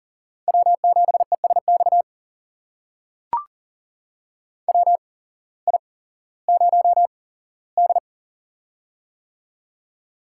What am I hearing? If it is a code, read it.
Morse code, heard as a keyed tone, about 40 wpm, spaced as for 19 wpm, 700 Hz. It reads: W7ESX WI0D
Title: USA Call Signs - 8x Character Spacing - 40wpm